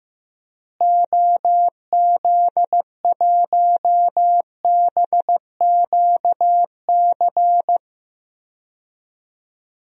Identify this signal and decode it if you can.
Morse code, heard as a keyed tone, about 15 wpm, 700 Hz. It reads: OZ1BQC
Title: DXCC-100 Call Signs - 6 Characters - Repeated 2x - 15wpm